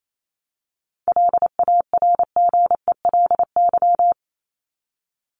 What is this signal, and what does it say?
Morse code, heard as a keyed tone, about 28 wpm, 700 Hz. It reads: LARGELY
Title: LY Suffix - Top 100 Words with Base Words 28wpm